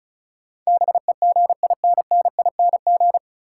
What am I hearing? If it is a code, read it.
Morse code, heard as a keyed tone, about 36 wpm, 700 Hz. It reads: BEGINNING
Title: Sets of 3 Words - New Word Drill - 35wpm